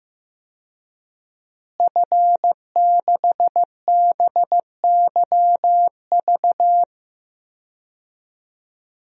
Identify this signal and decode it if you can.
Morse code, heard as a keyed tone, about 15 wpm, 700 Hz. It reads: F6BYV
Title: DXCC-100 Call Signs - 5 Characters - Repeated 3x - 15wpm